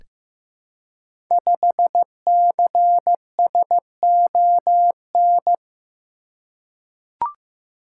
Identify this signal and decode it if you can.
Morse code, heard as a keyed tone, about 15 wpm, 700 Hz. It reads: 5CSON
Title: Code Groups - 5 Letter:Numbers - 15wpm